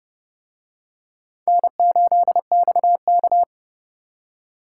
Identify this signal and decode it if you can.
Morse code, heard as a keyed tone, about 29 wpm, 700 Hz. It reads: N8XK